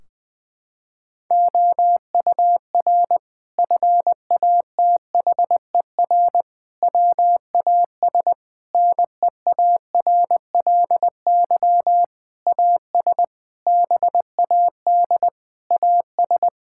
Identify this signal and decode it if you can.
Morse code, heard as a keyed tone, about 20 wpm, 700 Hz. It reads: OUR FATHER WAS NEARLY AS BAD AS
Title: Sentences from Top 1600 Words 20wpm